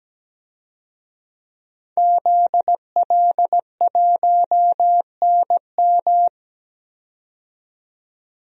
Morse code, heard as a keyed tone, about 17 wpm, 700 Hz. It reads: ZL1NM
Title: DXCC-100 Call Signs - 5 Characters - 17wpm